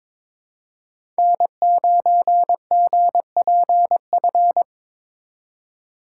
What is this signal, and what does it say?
Morse code, heard as a keyed tone, about 22 wpm, 700 Hz. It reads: N9GPF
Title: USA Call Signs - 5 Characters - 3x Character Spacing - 22wpm